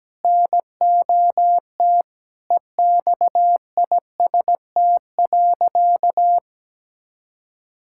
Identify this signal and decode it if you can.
Morse code, heard as a keyed tone, about 17 wpm, 700 Hz. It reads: NOT EXIST.
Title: Sentences from Top 1000 Words - Review Part 6 - 17wpm